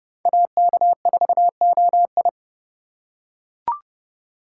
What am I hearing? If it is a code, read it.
Morse code, heard as a keyed tone, about 26 wpm, 700 Hz. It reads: AK4OI